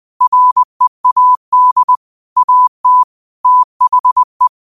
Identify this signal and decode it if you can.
Morse code, heard as a keyed tone, about 20 wpm, 1000 Hz. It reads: READ AT THE
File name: qotdfiles / 20wpm